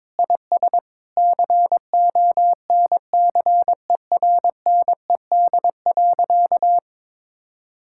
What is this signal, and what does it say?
Morse code, heard as a keyed tone, about 22 wpm, 700 Hz. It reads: IS CONCERNED.
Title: Sentences from Top 1000 Words - Review Part 6 - 22wpm